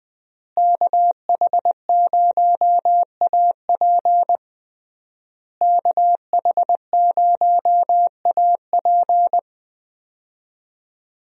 Morse code, heard as a keyed tone, about 20 wpm, 700 Hz. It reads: KH0AP KH0AP
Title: USA Call Signs - 5 Characters - Repeated 3x - 20wpm